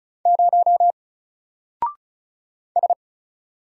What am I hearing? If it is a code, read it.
Morse code, heard as a keyed tone, about 35 wpm, 700 Hz. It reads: 0 S
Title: Single Character (Letter:Number:Punctuation:Prosign) 35wpm